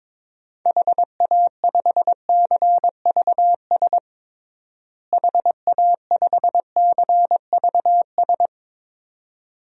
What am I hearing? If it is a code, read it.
Morse code, heard as a keyed tone, about 22 wpm, 700 Hz. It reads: HA5CVS HA5CVS
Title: DXCC-100 Call Signs - 6 Characters - Repeated 2x - 22wpm